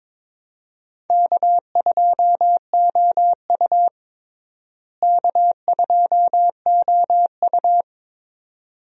Morse code, heard as a keyed tone, about 22 wpm, 700 Hz. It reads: K2OU K2OU
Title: USA Call Signs - 4 Characters - Repeated 2x - 22wpm